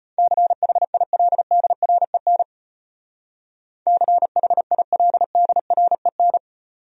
Morse code, heard as a keyed tone, about 32 wpm, 700 Hz. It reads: CHILDREN CHILDREN